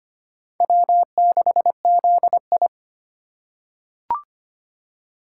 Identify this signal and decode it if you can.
Morse code, heard as a keyed tone, about 25 wpm, 700 Hz. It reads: W6ZI
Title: USA Call Signs - 4 Characters - 4x Character Spacing - 25wpm